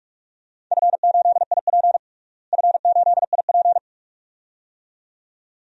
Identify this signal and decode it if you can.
Morse code, heard as a keyed tone, about 41 wpm, 700 Hz. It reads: F8IP F8IP
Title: DXCC-100 Call Signs - 4 Characters - Repeated 2x - 45wpm